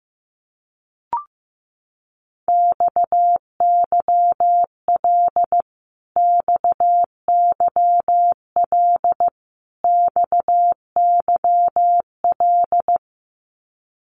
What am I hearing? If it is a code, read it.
Morse code, heard as a keyed tone, about 15 wpm, 700 Hz. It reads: XYL XYL XYL
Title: Abbreviations Repeated 3x - 15wpm